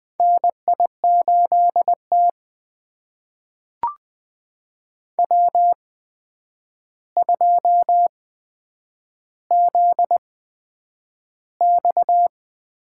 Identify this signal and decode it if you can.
Morse code, heard as a keyed tone, about 20 wpm, spaced as for 9 wpm, 700 Hz. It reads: NI8T W2ZX
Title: USA Call Signs - 4 Characters - 8x Character Spacing - 20wpm